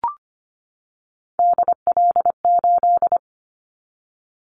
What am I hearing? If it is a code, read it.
Morse code, heard as a keyed tone, about 25 wpm, 700 Hz. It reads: DL8
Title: Callsign Prefixes for Top 50 DXCC - 25wpm